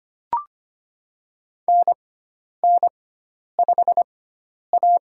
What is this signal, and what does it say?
Morse code, heard as a keyed tone, about 25 wpm, spaced as for 10 wpm, 700 Hz. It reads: NN5A
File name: Lesson 16g - Letter c - Call Signs 25wpm